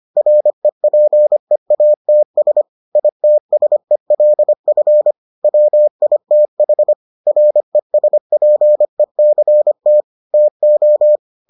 Morse code, heard as a keyed tone, about 25 wpm, 600 Hz. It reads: REPEATS ITSELF WITH RESPECT TO